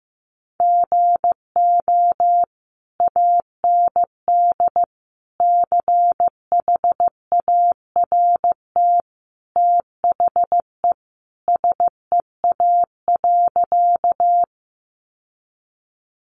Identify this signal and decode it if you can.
Morse code, heard as a keyed tone, about 15 wpm, 700 Hz. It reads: GO AND CHART THE SEA.
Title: Sentences from Top 1300 Words 15wpm